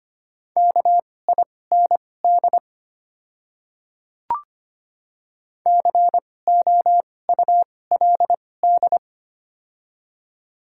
Morse code, heard as a keyed tone, about 25 wpm, spaced as for 18 wpm, 690 Hz. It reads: KIND COULD
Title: 2x Character Spacing - Top 200 Words - 25wpm